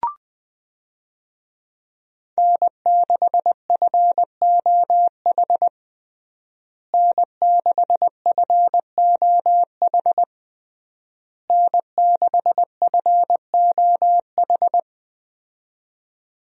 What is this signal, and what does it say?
Morse code, heard as a keyed tone, about 20 wpm, 700 Hz. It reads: N6FOH N6FOH N6FOH
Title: USA Call Signs - 5 Characters - Repeated 3x - 20wpm